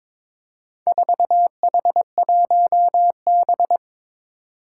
Morse code, heard as a keyed tone, about 21 wpm, 700 Hz. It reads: 4H1B
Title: Four Challenging Letter-Number Combinations 22wpm